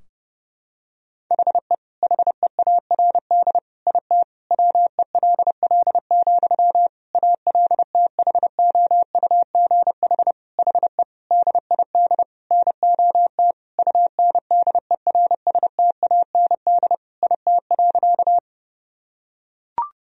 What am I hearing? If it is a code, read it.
Morse code, heard as a keyed tone, about 30 wpm, 700 Hz. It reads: HE HEARD IT WELL, ALTHOUGH HE DID NOT UNDERSTAND IT.
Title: Sentences from Top 1000 Words - Review Part 3 - 30wpm